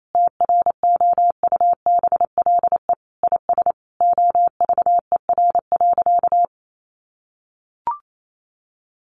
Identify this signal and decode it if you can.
Morse code, heard as a keyed tone, about 28 wpm, 700 Hz. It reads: TROUBLE IS OVER.
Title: Sentences from Top 1200 Words 28wpm